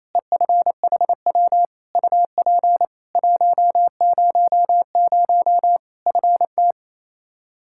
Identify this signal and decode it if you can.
Morse code, heard as a keyed tone, about 28 wpm, 700 Hz. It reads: EFHW UP 100 FT
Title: K1USN SST QSO Practice 28wpm